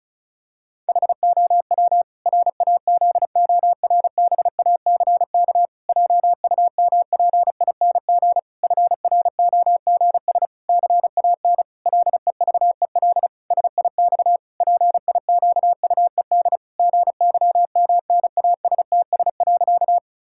Morse code, heard as a keyed tone, about 36 wpm, 700 Hz. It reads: HOW RAZORBACK JUMPING FROGS CAN LEVEL SIX PIQUED GYMNASTS.
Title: Pangram Challenge v2 - 35wpm